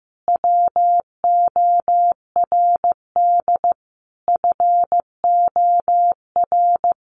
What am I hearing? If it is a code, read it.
Morse code, heard as a keyed tone, about 15 wpm, 700 Hz. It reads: WORD FOR
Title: Sentences from Top 1300 Words 15wpm